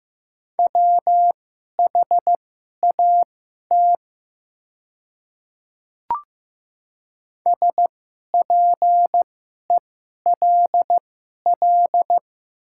Morse code, heard as a keyed tone, about 15 wpm, spaced as for 11 wpm, 700 Hz. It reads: WHAT SPELL
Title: 2x Character Spacing - Top 200 Words - 15wpm